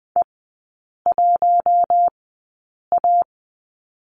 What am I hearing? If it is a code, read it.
Morse code, heard as a keyed tone, about 20 wpm, spaced as for 8 wpm, 700 Hz. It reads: E1A
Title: 053 Lesson 10e - Letter r - Three Characters 20wpm